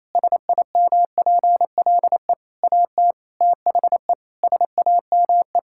Morse code, heard as a keyed tone, about 28 wpm, 700 Hz. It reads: SIMPLE AT THE SAME
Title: Sentences from Top 1000 Words - Review Part 6 - 28wpm